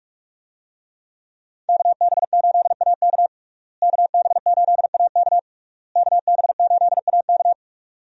Morse code, heard as a keyed tone, about 41 wpm, 700 Hz. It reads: KB8AK KB8AK KB8AK
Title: USA Call Signs - 5 Characters - Repeated 3x - 45wpm